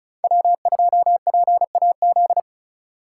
Morse code, heard as a keyed tone, about 36 wpm, 700 Hz. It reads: W2PAZ
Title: Speed Racing - USA Call Signs 30wpm